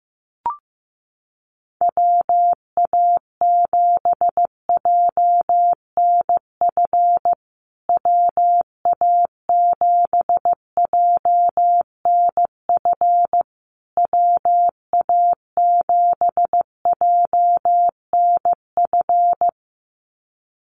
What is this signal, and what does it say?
Morse code, heard as a keyed tone, about 15 wpm, 700 Hz. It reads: WA7JNF WA7JNF WA7JNF
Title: USA Call Signs Repeated 3x 15wpm